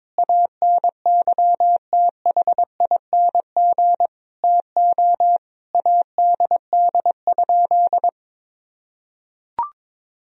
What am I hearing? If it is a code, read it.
Morse code, heard as a keyed tone, about 22 wpm, 700 Hz. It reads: ANYTHING TO ADD?
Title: Sentences from Top 1000 Words - Review Part 1 - 22wpm